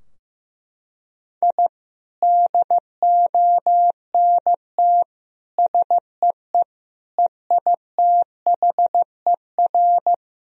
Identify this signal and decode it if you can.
Morse code, heard as a keyed tone, about 16 wpm, 700 Hz. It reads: I DONT SEE EITHER
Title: Sentences from Top 1000 Words - Review Part 4 - 15wpm